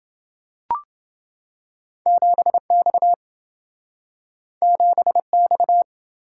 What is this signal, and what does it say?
Morse code, heard as a keyed tone, about 28 wpm, 700 Hz. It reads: 7X 7X